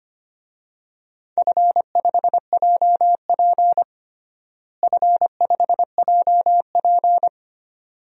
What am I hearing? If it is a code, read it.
Morse code, heard as a keyed tone, about 25 wpm, 700 Hz. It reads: F5JP F5JP
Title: DXCC-100 Call Signs - 4 Characters - Repeated 2x - 25wpm